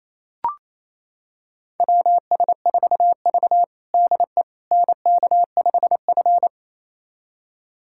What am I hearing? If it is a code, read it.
Morse code, heard as a keyed tone, about 27 wpm, 700 Hz. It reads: WS4V DE NK5F
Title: USA Callsign DE Callsign - 4 Characters - 28wpm